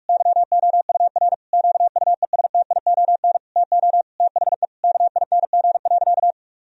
Morse code, heard as a keyed tone, about 41 wpm, 700 Hz. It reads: YOUR QUESTION TO THE KING.
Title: Sentences from Top 1000 Words - Review Part 2 - 45wpm